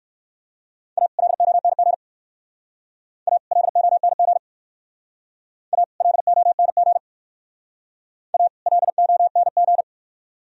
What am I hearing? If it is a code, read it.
Morse code, heard as a keyed tone, about 62 wpm, 700 Hz. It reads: A LONG A LONG A LONG A LONG